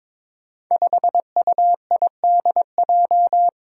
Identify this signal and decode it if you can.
Morse code, heard as a keyed tone, about 22 wpm, 700 Hz. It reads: 5UIDJ
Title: Five Challenging Letter-Number Combinations 22wpm